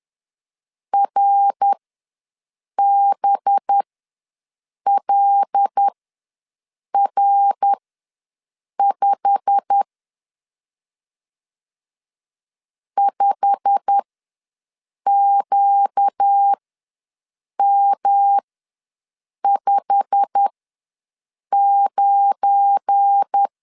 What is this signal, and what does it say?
Morse code, heard as a keyed tone, about 10 wpm, spaced as for 6 wpm, 800 Hz. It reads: RBLR5 5QM59
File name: LEZIONI / 8 YJ45 / 8 YJ45 ws 10wpm 3843